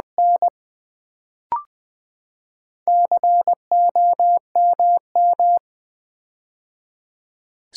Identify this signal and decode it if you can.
Morse code, heard as a keyed tone, about 20 wpm, 700 Hz. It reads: N COMM
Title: Build A Word - 4 Letters At A Time - 10-Letter Words - 20wpm